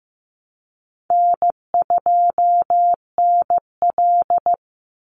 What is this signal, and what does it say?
Morse code, heard as a keyed tone, about 15 wpm, 700 Hz. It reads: N2NL
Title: Call Signs Repeated 2x - 15wpm